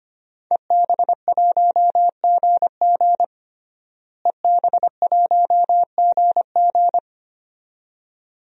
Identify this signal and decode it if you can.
Morse code, heard as a keyed tone, about 25 wpm, 700 Hz. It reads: EB1GG EB1GG
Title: DXCC-100 Call Signs - 5 Characters - Repeated 2x - 25wpm